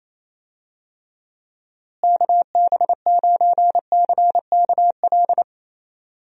Morse code, heard as a keyed tone, about 28 wpm, 700 Hz. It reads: KB9CKL
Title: USA Call Signs - 6 Characters - Repeated 2x - 28wpm